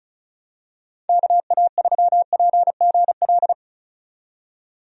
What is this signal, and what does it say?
Morse code, heard as a keyed tone, about 36 wpm, 700 Hz. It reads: KA3PGL
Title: USA Call Signs - 6 Characters - 35wpm